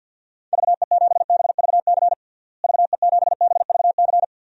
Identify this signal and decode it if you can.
Morse code, heard as a keyed tone, about 52 wpm, 700 Hz. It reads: VE7BVC VE7BVC
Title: DXCC-100 Call Signs - 6 Characters - Repeated 2x - 50wpm